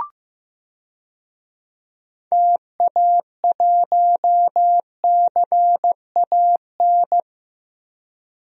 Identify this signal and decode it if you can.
Morse code, heard as a keyed tone, about 15 wpm, 700 Hz. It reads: TA1CAN
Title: DXCC-100 Call Signs - 6 Characters - 15wpm